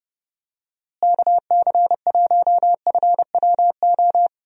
Morse code, heard as a keyed tone, about 32 wpm, 700 Hz. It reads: KC1FWO
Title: USA Call Signs - 6 Characters - Repeated 3x - 30wpm